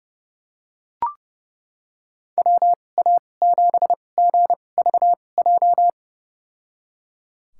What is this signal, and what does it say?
Morse code, heard as a keyed tone, about 31 wpm, spaced as for 22 wpm, 700 Hz. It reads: WA7GVJ
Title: USA Call Signs - 2x Character Spacing - 30wpm